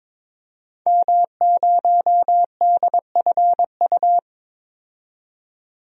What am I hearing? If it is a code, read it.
Morse code, heard as a keyed tone, about 22 wpm, 700 Hz. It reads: M0DFU
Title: DXCC-100 Call Signs - 5 Characters - 22wpm